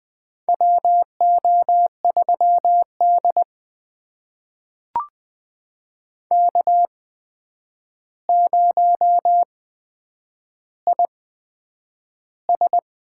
Morse code, heard as a keyed tone, about 20 wpm, spaced as for 9 wpm, 700 Hz. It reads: WO3D K0IS
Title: USA Call Signs - 4 Characters - 8x Character Spacing - 20wpm